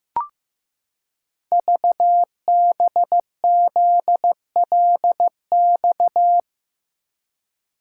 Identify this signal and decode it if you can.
Morse code, heard as a keyed tone, about 15 wpm, 700 Hz. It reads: VBZLX
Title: Code Groups - 5 Letters - 15wpm